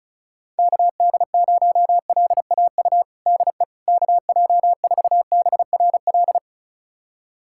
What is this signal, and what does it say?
Morse code, heard as a keyed tone, about 34 wpm, 700 Hz. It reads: KD0LAU DE KJ4BRL
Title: USA Callsign DE Callsign 35wpm